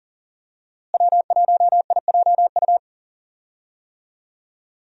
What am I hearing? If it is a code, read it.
Morse code, heard as a keyed tone, about 40 wpm, 700 Hz. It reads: W1IJU
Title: USA Call Signs - 5 Characters - Repeated 3x - 40wpm